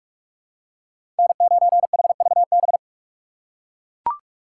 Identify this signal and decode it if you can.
Morse code, heard as a keyed tone, about 42 wpm, 700 Hz. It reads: N9HVB
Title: USA Call Signs - 6x Character Spacing - 45wpm